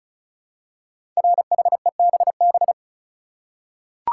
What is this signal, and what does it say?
Morse code, heard as a keyed tone, about 36 wpm, 700 Hz. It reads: RHEBB
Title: Code Groups - 5 Letters - 35wpm